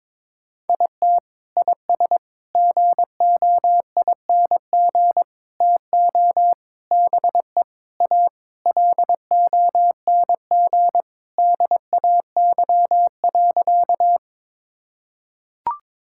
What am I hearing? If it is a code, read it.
Morse code, heard as a keyed tone, about 22 wpm, 700 Hz. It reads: IT IS GOING TO BE A LONG DAY.
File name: practice / 22wpm / Sentences with 4-Word Phrases - 22wpm